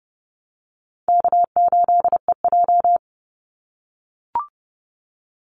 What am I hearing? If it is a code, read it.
Morse code, heard as a keyed tone, about 30 wpm, 700 Hz. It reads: K8EJ
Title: Easy Call Signs - 30wpm